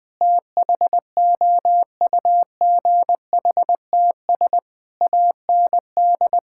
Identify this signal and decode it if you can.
Morse code, heard as a keyed tone, about 20 wpm, 700 Hz. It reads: THOUGHTS AND
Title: Sentences with 8-Word Phrases - 20wpm